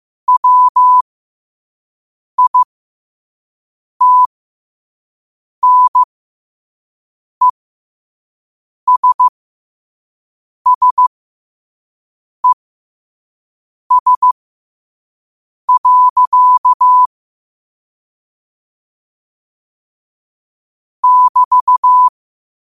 Morse code, heard as a keyed tone, about 15 wpm, spaced as for 5 wpm, 1000 Hz. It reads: WITNESSES. <BT>
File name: qotdfiles / 5wpm